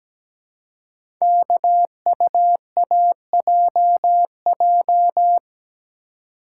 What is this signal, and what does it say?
Morse code, heard as a keyed tone, about 17 wpm, 700 Hz. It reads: KUAJJ
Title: Code Groups - 5 Letters - 17wpm